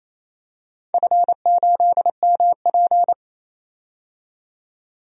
Morse code, heard as a keyed tone, about 28 wpm, 700 Hz. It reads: F8MP
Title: DXCC-100 Call Signs - 4 Characters - 28wpm